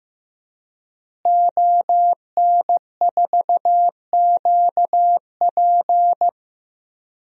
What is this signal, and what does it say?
Morse code, heard as a keyed tone, about 15 wpm, 700 Hz. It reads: ON4QP
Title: DXCC-100 Call Signs - 5 Characters - Repeated 2x - 15wpm